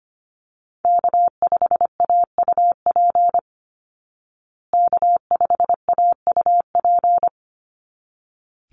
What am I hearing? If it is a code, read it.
Morse code, heard as a keyed tone, about 25 wpm, 700 Hz. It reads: K5AUP K5AUP
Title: Call Signs Repeated 3x - 25wpm